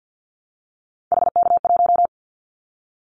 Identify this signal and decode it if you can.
Morse code, heard as a keyed tone, about 49 wpm, 700 Hz. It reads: HK1
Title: Callsign Prefixes for Top 100 DXCC - 50wpm